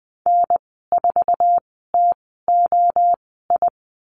Morse code, heard as a keyed tone, about 20 wpm, spaced as for 14 wpm, 700 Hz. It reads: N4TOI
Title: 082 Lesson 14g - Number 2 - Call Signs 20wpm